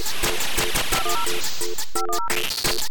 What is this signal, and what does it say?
DTMF, heard as telephone touch tones, tones of about 110 ms. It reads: 10 510
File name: kandi-hippie kandi-hippie---dial-up-glitchtune